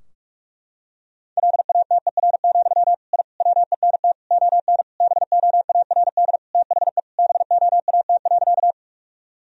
Morse code, heard as a keyed tone, about 46 wpm, 700 Hz. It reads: LATER, I WENT ON BOARD THE BOAT.
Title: Sentences from Top 1000 Words - Review Part 4 - 45wpm